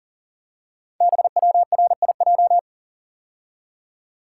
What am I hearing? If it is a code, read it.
Morse code, heard as a keyed tone, about 42 wpm, 700 Hz. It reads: BWRIJ
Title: Code Groups - 5 Letters - 40wpm